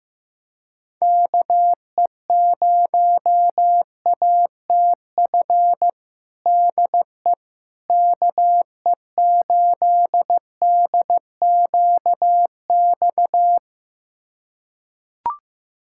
Morse code, heard as a keyed tone, about 15 wpm, 700 Hz. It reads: KE0ATF DE KE8DQX
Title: USA Callsign DE Callsign 15wpm